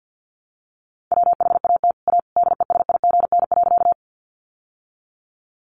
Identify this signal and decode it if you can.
Morse code, heard as a keyed tone, about 50 wpm, 700 Hz. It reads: WHAT A DESIGN.
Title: Sentences from Top 800 Words 50wpm